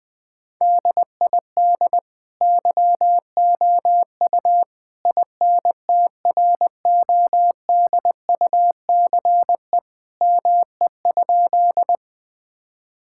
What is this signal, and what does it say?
Morse code, heard as a keyed tone, about 20 wpm, 700 Hz. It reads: DID YOU INTRODUCE ME?
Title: Sentences from Top 1600 Words 20wpm